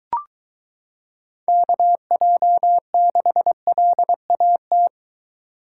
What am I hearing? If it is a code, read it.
Morse code, heard as a keyed tone, about 23 wpm, 700 Hz. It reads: KJ6LAT